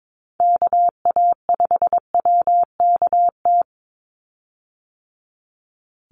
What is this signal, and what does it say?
Morse code, heard as a keyed tone, about 22 wpm, 700 Hz. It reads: KA5WKT
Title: USA Call Signs 22wpm